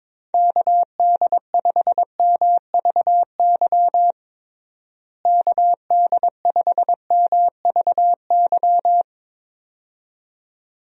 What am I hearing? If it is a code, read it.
Morse code, heard as a keyed tone, about 22 wpm, 700 Hz. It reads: KD5MVY KD5MVY
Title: USA Call Signs - 6 Characters - Repeated 2x - 22wpm